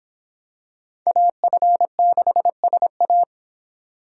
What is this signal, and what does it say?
Morse code, heard as a keyed tone, about 26 wpm, 700 Hz. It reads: AF6SA